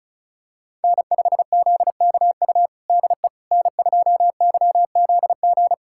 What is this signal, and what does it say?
Morse code, heard as a keyed tone, about 36 wpm, 700 Hz. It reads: N5ZKU DE N2YZG
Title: USA Callsign DE Callsign - 5 Characters - 35wpm